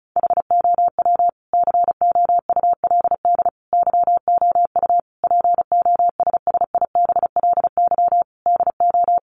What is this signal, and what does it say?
Morse code, heard as a keyed tone, about 34 wpm, 700 Hz. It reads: HOW COULD YOU POSSIBLY DO
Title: Sentences from Top 1300 Words 35wpm